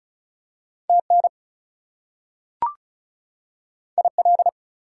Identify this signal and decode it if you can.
Morse code, heard as a keyed tone, about 36 wpm, 700 Hz. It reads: TN IL
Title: US States and Canadian Provinces 35wpm